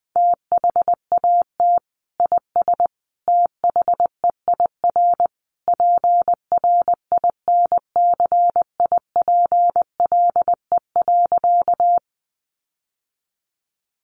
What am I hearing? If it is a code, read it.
Morse code, heard as a keyed tone, about 20 wpm, 700 Hz. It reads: THAT IS THEIR PRINCIPLE.
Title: Sentences from Top 1400 Words 20wpm